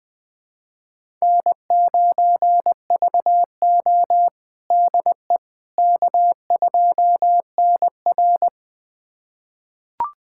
Morse code, heard as a keyed tone, about 20 wpm, 700 Hz. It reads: N9VO DE K2NR
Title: USA Callsign DE Callsign - 4 Characters - 20wpm